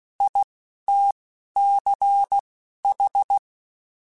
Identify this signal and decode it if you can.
Morse code, heard as a keyed tone, about 16 wpm, spaced as for 11 wpm, 800 Hz. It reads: ITCH
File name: Alphabetical / Alphabetical mp3s / Is @10